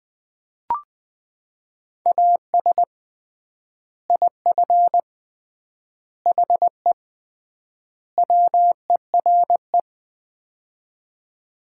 Morse code, heard as a keyed tone, about 20 wpm, 700 Hz. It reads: AS IF HE WERE
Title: 3x Word-Spacing - Sets of 4 Words - 20wpm